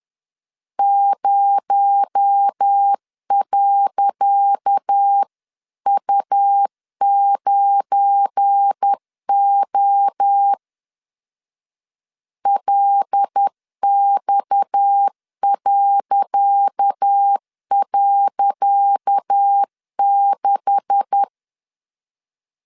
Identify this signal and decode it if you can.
Morse code, heard as a keyed tone, about 10 wpm, 800 Hz. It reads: 0.U9O LX..6
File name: LEZIONI / 10 SIGNS / 10C DOT 10wpm 2940